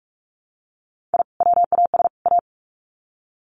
Morse code, heard as a keyed tone, about 44 wpm, 700 Hz. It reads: I WAS A
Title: Sets of 3 Words 45wpm